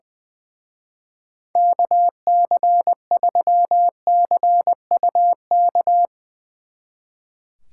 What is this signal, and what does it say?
Morse code, heard as a keyed tone, about 22 wpm, 700 Hz. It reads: KC3CUK
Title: Speed Racing - USA 2x3 Call Signs 20wpm